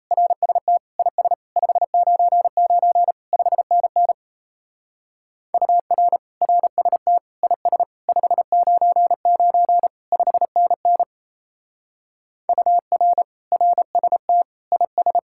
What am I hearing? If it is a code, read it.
Morse code, heard as a keyed tone, about 37 wpm, 700 Hz. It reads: RST IS 599 5NN UR RST IS 599 5NN UR RST IS